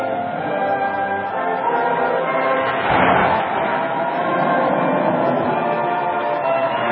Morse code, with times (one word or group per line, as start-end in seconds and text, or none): none